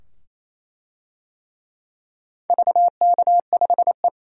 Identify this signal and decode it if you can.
Morse code, heard as a keyed tone, about 28 wpm, 700 Hz. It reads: VK5E
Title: DXCC-100 Call Signs - 4 Characters - 28wpm